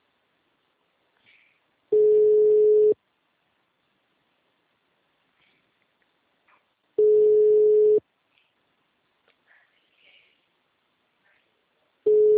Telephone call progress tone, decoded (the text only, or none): ringback tone